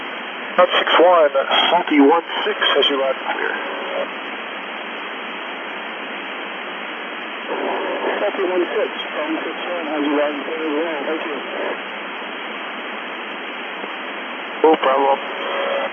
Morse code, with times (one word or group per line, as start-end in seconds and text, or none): none